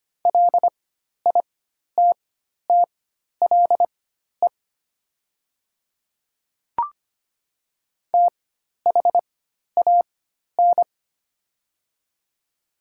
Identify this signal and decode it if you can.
Morse code, heard as a keyed tone, about 25 wpm, spaced as for 12 wpm, 700 Hz. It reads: LITTLE THAN
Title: 4x Character Spacing - Top 300 Words - 25wpm